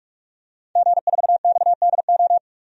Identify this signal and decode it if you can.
Morse code, heard as a keyed tone, about 46 wpm, 700 Hz. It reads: G4XDO
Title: DXCC-100 Call Signs - 5 Characters - 45wpm